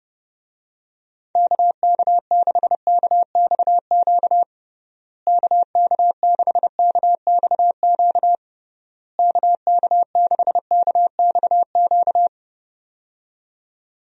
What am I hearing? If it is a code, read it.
Morse code, heard as a keyed tone, about 30 wpm, 700 Hz. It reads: KK6KXQ KK6KXQ KK6KXQ
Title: USA Call Signs - 6 Characters - Repeated 3x - 30wpm